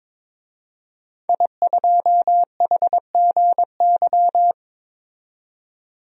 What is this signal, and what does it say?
Morse code, heard as a keyed tone, about 22 wpm, 700 Hz. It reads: I2HGY
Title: DXCC-100 Call Signs - 5 Characters - 22wpm